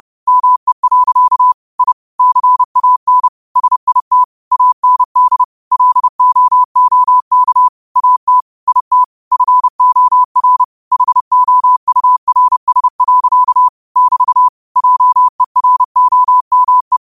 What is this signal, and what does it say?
Morse code, heard as a keyed tone, about 30 wpm, 1000 Hz. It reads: ME. I CAN SIT AND LOOK AT IT FOR HOURS. <BT> JEROME
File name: qotdfiles / 30wpm